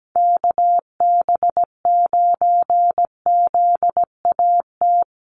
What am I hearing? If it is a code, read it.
Morse code, heard as a keyed tone, about 17 wpm, 700 Hz. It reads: KB9ZAT
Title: USA Call Signs Repeated 3x 17wpm